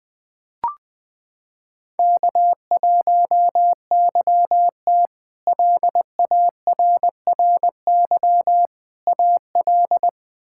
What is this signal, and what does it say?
Morse code, heard as a keyed tone, about 20 wpm, 700 Hz. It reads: K1YT LARRY AL
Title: K1USN SST QSO Practice - 20wpm